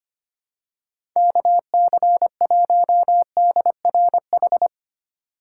Morse code, heard as a keyed tone, about 25 wpm, 700 Hz. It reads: KC1DRH
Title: USA Call Signs - 6 Characters - 4x Character Spacing - 25wpm